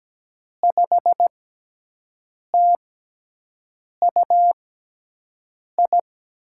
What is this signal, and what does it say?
Morse code, heard as a keyed tone, about 17 wpm, spaced as for 6 wpm, 700 Hz. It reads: 5TUI
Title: USA Call Signs - 5 Characters - 6x Character Spacing - 17wpm